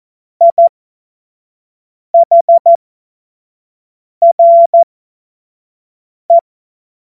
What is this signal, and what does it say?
Morse code, heard as a keyed tone, about 14 wpm, spaced as for 5 wpm, 680 Hz. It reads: IHRE
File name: Morsekurs / Witze / Witz 08. November 2022 25 F70